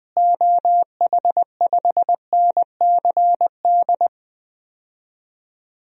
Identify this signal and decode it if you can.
Morse code, heard as a keyed tone, about 20 wpm, 700 Hz. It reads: OH5NCD
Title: DXCC-100 Call Signs - 6 Characters - Repeated 3x - 20wpm